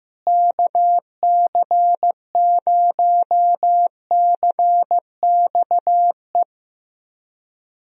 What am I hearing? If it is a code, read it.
Morse code, heard as a keyed tone, about 15 wpm, 700 Hz. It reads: KC0CXE
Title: USA Call Signs - 6 Characters - Repeated 2x - 15wpm